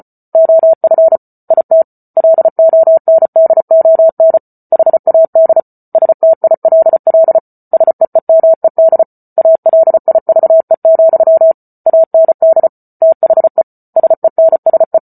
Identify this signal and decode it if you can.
Morse code, heard as a keyed tone, about 34 wpm, 650 Hz. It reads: OF IT LONDON HAD STILL SEEMED ALIVE, AND THE SENSE